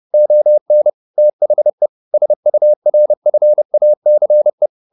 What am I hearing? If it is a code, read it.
Morse code, heard as a keyed tone, about 30 wpm, 600 Hz. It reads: ON THE SURFACE
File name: the old man and the sea / the old man and the sea 1 30wpm